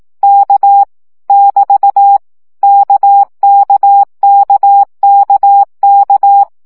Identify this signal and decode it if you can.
Morse code, heard as a keyed tone, about 18 wpm, 800 Hz. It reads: K <BT> KKKKK
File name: CW gyakorlas k 15 baud